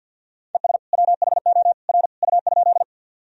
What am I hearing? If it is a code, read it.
Morse code, heard as a keyed tone, about 47 wpm, 700 Hz. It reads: ES WHO R U?
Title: Sentences from Top 100 Words and Abbreviations 50wpm